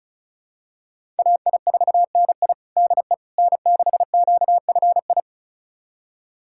Morse code, heard as a keyed tone, about 34 wpm, 700 Hz. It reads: AI4NI DE N6QFI
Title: USA Callsign DE Callsign - 5 Characters - 35wpm